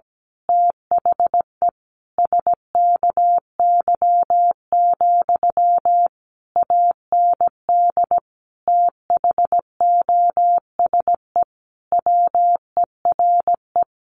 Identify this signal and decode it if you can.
Morse code, heard as a keyed tone, about 17 wpm, 700 Hz. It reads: THE SKY, AND THOSE WERE
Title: A Sound Of Thunder 17wpm